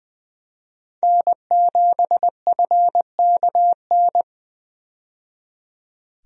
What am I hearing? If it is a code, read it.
Morse code, heard as a keyed tone, about 20 wpm, 700 Hz. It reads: N7FKN
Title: USA Call Signs - 5 Characters - 20wpm